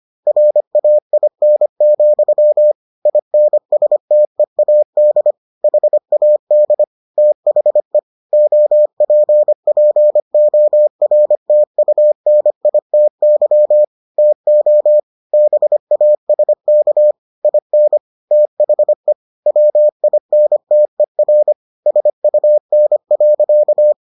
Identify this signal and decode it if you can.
Morse code, heard as a keyed tone, about 25 wpm, 600 Hz. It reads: RAIN, INSTEAD HAD THE OPPORTUNITY TO BASK IN THE WINTER SUN.